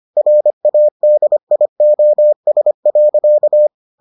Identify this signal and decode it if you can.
Morse code, heard as a keyed tone, about 25 wpm, 600 Hz. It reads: RADIOS.